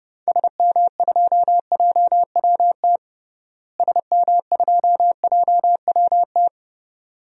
Morse code, heard as a keyed tone, about 31 wpm, 700 Hz. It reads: SM2JWT SM2JWT
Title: DXCC-100 Call Signs - 6 Characters - Repeated 2x - 30wpm